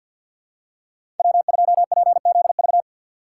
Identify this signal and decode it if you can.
Morse code, heard as a keyed tone, about 47 wpm, 700 Hz. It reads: W2PZV
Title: USA Call Signs - 5 Characters - 6x Character Spacing - 50wpm